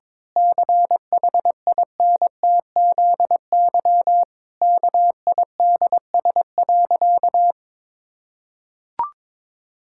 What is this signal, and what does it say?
Morse code, heard as a keyed tone, about 22 wpm, 700 Hz. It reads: CHINTZY KIDS.
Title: Pangram Challenge v2 - 22wpm